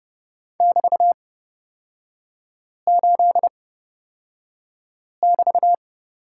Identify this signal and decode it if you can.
Morse code, heard as a keyed tone, about 30 wpm, spaced as for 5 wpm, 700 Hz. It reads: <BT>8<BT>
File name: Lesson 36e - BT - Three Characters 30wpm